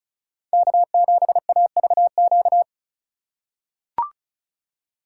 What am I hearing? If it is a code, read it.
Morse code, heard as a keyed tone, about 36 wpm, 700 Hz. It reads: K7AVQ
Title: USA Call Signs - 5 Characters - 3x Character Spacing - 35wpm